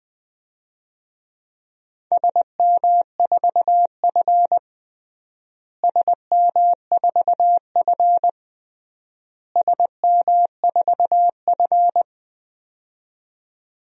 Morse code, heard as a keyed tone, about 21 wpm, 700 Hz. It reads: SM4F SM4F SM4F
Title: DXCC-100 Call Signs - 4 Characters - Repeated 3x - 20wpm